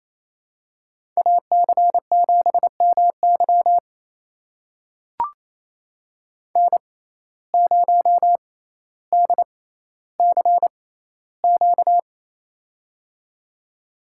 Morse code, heard as a keyed tone, about 28 wpm, spaced as for 14 wpm, 700 Hz. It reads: AC7MY N0DCQ
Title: USA Call Signs - 5 Characters - 6x Character Spacing - 28wpm